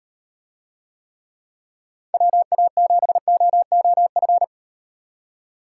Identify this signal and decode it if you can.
Morse code, heard as a keyed tone, about 38 wpm, 700 Hz. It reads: WA7OOF